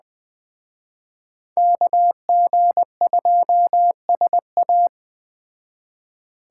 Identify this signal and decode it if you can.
Morse code, heard as a keyed tone, about 20 wpm, 700 Hz. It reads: KG2SA